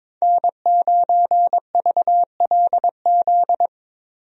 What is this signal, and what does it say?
Morse code, heard as a keyed tone, about 22 wpm, 700 Hz. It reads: N9VLZ
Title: USA Call Signs - 5 Characters - Repeated 2x - 22wpm